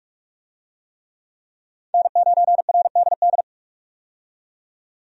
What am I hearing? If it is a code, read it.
Morse code, heard as a keyed tone, about 44 wpm, 700 Hz. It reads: N9RDD
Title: USA Call Signs - 5 Characters - 45wpm